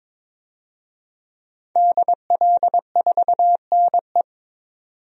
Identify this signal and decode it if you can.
Morse code, heard as a keyed tone, about 22 wpm, 700 Hz. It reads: DL4NE
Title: DXCC-100 Call Signs - 5 Characters - 22wpm